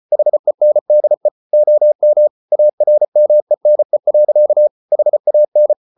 Morse code, heard as a keyed tone, about 34 wpm, 600 Hz. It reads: HENDE OM ARMENE. HAN
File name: Ferie QRQ 34wpm